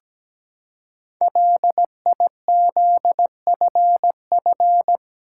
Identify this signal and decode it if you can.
Morse code, heard as a keyed tone, about 17 wpm, 700 Hz. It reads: LIZFF
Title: Code Groups - 5 Letters - 17wpm